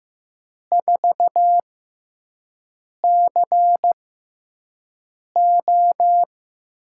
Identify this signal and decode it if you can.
Morse code, heard as a keyed tone, about 16 wpm, spaced as for 5 wpm, 700 Hz. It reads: 4CO
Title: USA Call Signs - 6 Characters - 6x Character Spacing - 15wpm